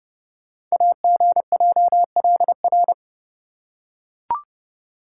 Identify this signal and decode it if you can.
Morse code, heard as a keyed tone, about 30 wpm, 700 Hz. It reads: AGJLR
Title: Code Groups - 5 Letters - 30wpm